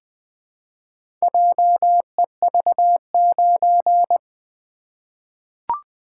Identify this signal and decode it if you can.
Morse code, heard as a keyed tone, about 20 wpm, 700 Hz. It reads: JEV9
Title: Four Letter-Number Combinations - 20wpm